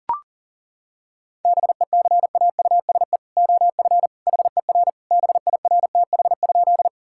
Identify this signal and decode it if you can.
Morse code, heard as a keyed tone, about 40 wpm, 700 Hz. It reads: BECAUSE OF HER BIRTH?
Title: Sentences from Top 1000 Words - Review Part 5 - 40wpm